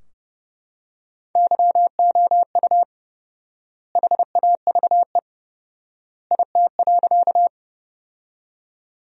Morse code, heard as a keyed tone, about 30 wpm, 700 Hz. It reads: YOU HAVE IT.
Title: 4x Word-Spacing - Sentences from Top 100 Words - 30wpm